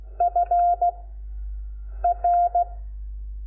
Morse code, heard as a keyed tone, about 16 wpm, spaced as for 6 wpm, 700 Hz. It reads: FR
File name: Lecciones / CLASE Nº5